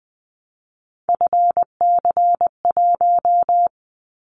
Morse code, heard as a keyed tone, about 20 wpm, 700 Hz. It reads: FC1
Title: Callsign Prefixes for Top 50 DXCC - 20wpm